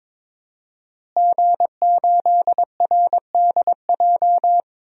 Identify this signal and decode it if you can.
Morse code, heard as a keyed tone, about 22 wpm, 700 Hz. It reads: G8RDJ
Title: DXCC-100 Call Signs - 5 Characters - Repeated 2x - 22wpm